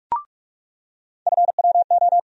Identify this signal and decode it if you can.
Morse code, heard as a keyed tone, about 45 wpm, 700 Hz. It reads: FWO
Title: USA Call Sign Postfixes - 3 Letters 45wpm